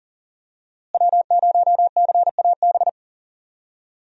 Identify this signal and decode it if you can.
Morse code, heard as a keyed tone, about 40 wpm, 700 Hz. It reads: W0CAB
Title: USA Call Signs - 5 Characters - Repeated 3x - 40wpm